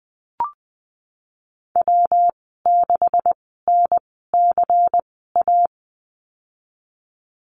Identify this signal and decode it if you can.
Morse code, heard as a keyed tone, about 20 wpm, spaced as for 15 wpm, 700 Hz. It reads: W6NCA
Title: 213 Lesson 34g - Number 8 - Call Signs 20wpm